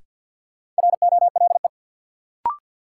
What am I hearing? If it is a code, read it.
Morse code, heard as a keyed tone, about 53 wpm, 700 Hz. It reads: ROLE
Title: Sentences from Top 1000 Words - Review Part 6 - New Word Drill - 50wpm